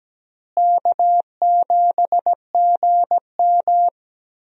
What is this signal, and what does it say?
Morse code, heard as a keyed tone, about 17 wpm, 700 Hz. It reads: K7GM
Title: USA Call Signs - 4 Characters - Repeated 2x - 17wpm